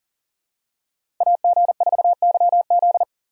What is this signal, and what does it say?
Morse code, heard as a keyed tone, about 40 wpm, 700 Hz. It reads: AG4YZ
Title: USA Call Signs - 5 Characters - Repeated 2x - 40wpm